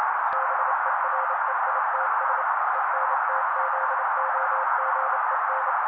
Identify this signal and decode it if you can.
Morse code, heard as a keyed tone, about 28 wpm, 530 Hz. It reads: BEREITS ENTZOGEN
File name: daily / 27wpm noise